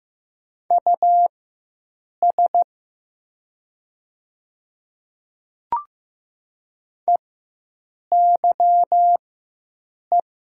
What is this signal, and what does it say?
Morse code, heard as a keyed tone, about 15 wpm, spaced as for 7 wpm, 700 Hz. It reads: US EYE
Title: 4x Character Spacing - Top 300 Words - 15wpm